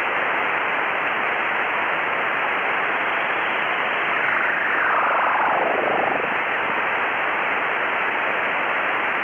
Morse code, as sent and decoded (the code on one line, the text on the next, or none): none